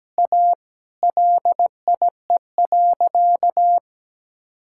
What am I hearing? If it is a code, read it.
Morse code, heard as a keyed tone, about 17 wpm, 700 Hz. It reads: A LIE.
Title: Sentences from Top 1600 Words 17wpm